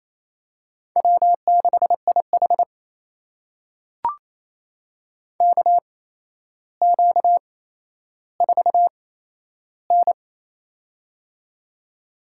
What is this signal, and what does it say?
Morse code, heard as a keyed tone, about 28 wpm, spaced as for 12 wpm, 700 Hz. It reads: W6IH KQ4N
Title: USA Call Signs - 4 Characters - 8x Character Spacing - 28wpm